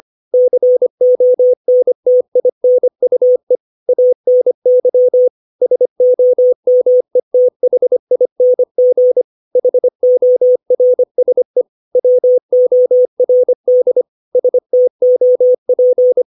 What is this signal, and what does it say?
Morse code, heard as a keyed tone, about 25 wpm, 500 Hz. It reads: CONTINUE ANY SOMETHING HORSE WORD STOP
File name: 632 words / 632 Words 25WPM